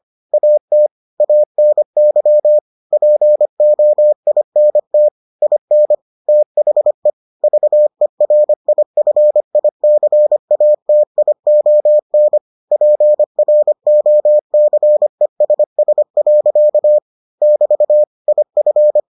Morse code, heard as a keyed tone, about 25 wpm, 600 Hz. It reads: AT ANY POINT IN THE VERIFICATION PROCESS. <BT> IF